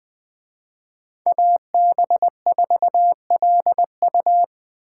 Morse code, heard as a keyed tone, about 20 wpm, 700 Hz. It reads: AB4LU
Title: Speed Racing - USA 2x2 Call Signs 17wpm